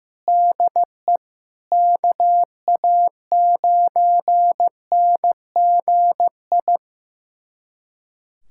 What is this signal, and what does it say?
Morse code, heard as a keyed tone, about 15 wpm, 700 Hz. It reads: DE KA9NGI
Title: USA Callsign DE Callsign 15wpm